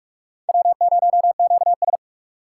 Morse code, heard as a keyed tone, about 41 wpm, 700 Hz. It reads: W0QS